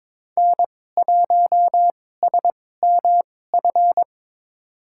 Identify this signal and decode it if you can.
Morse code, heard as a keyed tone, about 22 wpm, spaced as for 16 wpm, 700 Hz. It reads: N1SMF
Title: USA Call Signs - 5 Characters - 2x Character Spacing - 22wpm